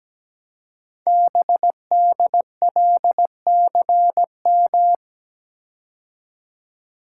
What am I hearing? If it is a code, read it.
Morse code, heard as a keyed tone, about 17 wpm, 700 Hz. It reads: BDLCM
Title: Code Groups - 5 Letters - 17wpm